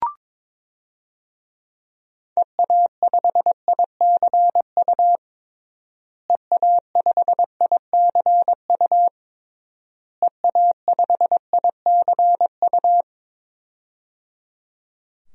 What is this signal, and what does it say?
Morse code, heard as a keyed tone, about 22 wpm, 700 Hz. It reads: EA5ICU EA5ICU EA5ICU
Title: DXCC-100 Call Signs - 6 Characters - Repeated 3x - 22wpm